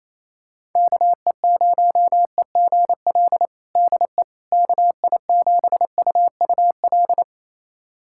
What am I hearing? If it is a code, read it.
Morse code, heard as a keyed tone, about 28 wpm, 700 Hz. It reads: KE0EGL DE KI7UUL
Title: USA Callsign DE Callsign 28wpm